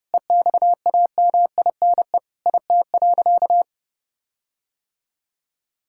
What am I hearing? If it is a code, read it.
Morse code, heard as a keyed tone, about 30 wpm, 700 Hz. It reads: EXAMINE IT.
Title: Sentences from Top 1600 Words 30wpm